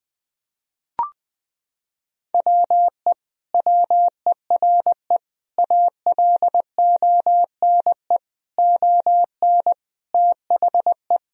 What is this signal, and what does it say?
Morse code, heard as a keyed tone, about 20 wpm, 700 Hz. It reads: WE WERE ALONE ON THE
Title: Sentences from Top 1000 Words - Review Part 3 - 20wpm